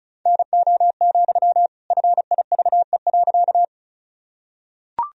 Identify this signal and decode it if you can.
Morse code, heard as a keyed tone, about 36 wpm, 700 Hz. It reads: NO, FIVE.
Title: Sentences from Top 1000 Words - Review Part 2 - 35wpm